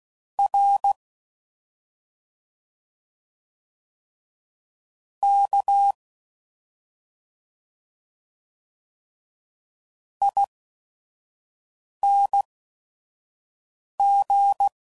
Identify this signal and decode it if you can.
Morse code, heard as a keyed tone, about 16 wpm, spaced as for 5 wpm, 800 Hz. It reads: R K ING